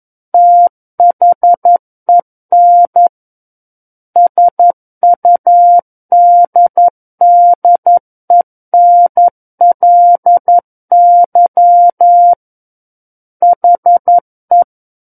Morse code, heard as a keyed tone, about 11 wpm, 700 Hz. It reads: THEN SUDDENLY HE
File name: 11 WPM